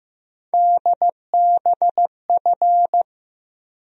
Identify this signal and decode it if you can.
Morse code, heard as a keyed tone, about 15 wpm, 700 Hz. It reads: DBF
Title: USA Call Sign Postfixes - 3 Letters 15wpm